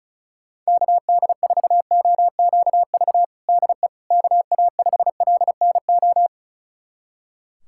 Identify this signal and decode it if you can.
Morse code, heard as a keyed tone, about 36 wpm, 700 Hz. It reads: KD4OQV DE KA5LNO
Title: USA Callsign DE Callsign - 6 Characters - 35wpm